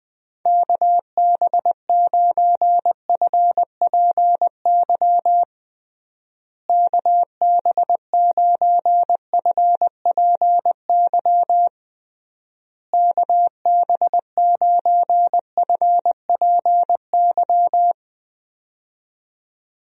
Morse code, heard as a keyed tone, about 20 wpm, 700 Hz. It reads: KB9FPY KB9FPY KB9FPY
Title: USA Call Signs - 6 Characters - Repeated 3x - 20wpm